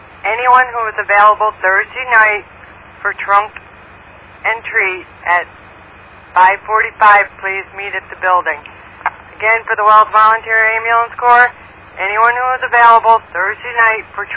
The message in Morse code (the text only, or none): none